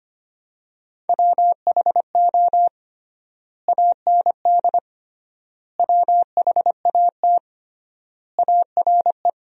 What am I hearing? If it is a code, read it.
Morse code, heard as a keyed tone, about 25 wpm, 700 Hz. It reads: WHO AND WHAT ARE
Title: 3x Word-Spacing - Sentences from Top 100 Words - 25wpm